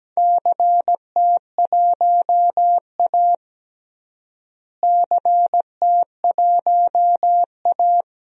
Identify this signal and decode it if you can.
Morse code, heard as a keyed tone, about 17 wpm, 700 Hz. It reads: CT1A CT1A
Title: DXCC-100 Call Signs - 4 Characters - Repeated 3x - 17wpm